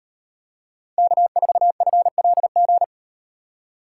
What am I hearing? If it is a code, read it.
Morse code, heard as a keyed tone, about 38 wpm, 700 Hz. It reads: K4FLG